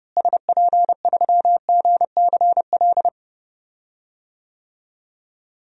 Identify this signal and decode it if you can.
Morse code, heard as a keyed tone, about 30 wpm, 700 Hz. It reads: SP3GCL
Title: DXCC-100 Call Signs - 6 Characters - Repeated 2x - 30wpm